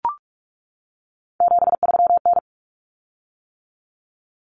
Morse code, heard as a keyed tone, about 44 wpm, 700 Hz. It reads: 73N
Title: Common Cut Numbers - 3-Digit 45wpm